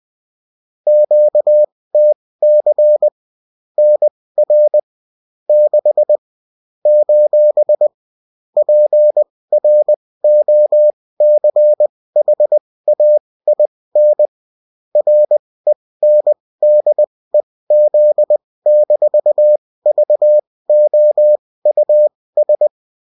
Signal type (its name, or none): Morse code